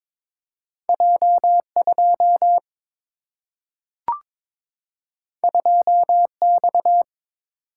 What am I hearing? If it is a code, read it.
Morse code, heard as a keyed tone, about 22 wpm, 700 Hz. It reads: J2 2X
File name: practice / 22wpm / Two Letter-Number Combinations - 22wpm